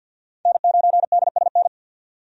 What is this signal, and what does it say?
Morse code, heard as a keyed tone, about 52 wpm, 700 Hz. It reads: N9DSN
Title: USA Call Signs - 5 Characters - 50wpm